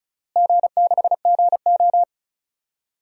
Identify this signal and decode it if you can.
Morse code, heard as a keyed tone, about 34 wpm, 700 Hz. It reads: G6GO
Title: DXCC-100 Call Signs - 4 Characters - Repeated 3x - 35wpm